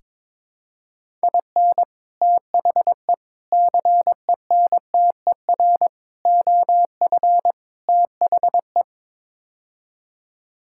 Morse code, heard as a keyed tone, about 21 wpm, 700 Hz. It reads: IN THE CENTER OF THE
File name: practice / 22wpm / Build A Phrase - 5-Word Phrases - 22wpm